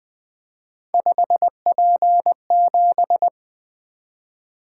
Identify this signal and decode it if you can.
Morse code, heard as a keyed tone, about 20 wpm, 700 Hz. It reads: 5P7
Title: Three Letter-Number Combinations - 20wpm